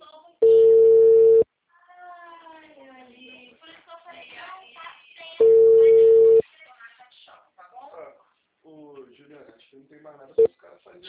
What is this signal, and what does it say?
Ringback tone (China, 450 Hz), with speech.